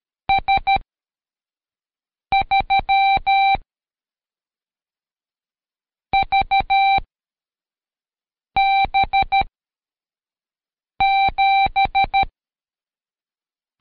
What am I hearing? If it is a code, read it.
Morse code, heard as a keyed tone, about 13 wpm, 780 Hz. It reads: S 3 V B 7
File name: Code Quick / AudioDownld2 / Seg104-Day12-FiinalExam 1